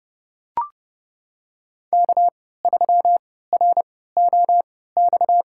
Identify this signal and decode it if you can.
Morse code, heard as a keyed tone, about 30 wpm, spaced as for 17 wpm, 700 Hz. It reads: K3ROX
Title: USA Call Signs - 3x Character Spacing - 30wpm